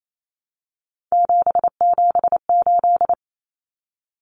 Morse code, heard as a keyed tone, about 28 wpm, 700 Hz. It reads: 778
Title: Numbers - 3-Digit 28wpm